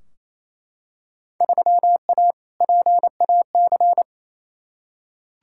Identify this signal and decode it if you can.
Morse code, heard as a keyed tone, about 28 wpm, 700 Hz. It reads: 3A PAC
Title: Field Day Exchanges Encore - 28wpm